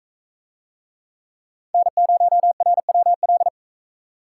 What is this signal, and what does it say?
Morse code, heard as a keyed tone, about 40 wpm, 700 Hz. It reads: N0RWL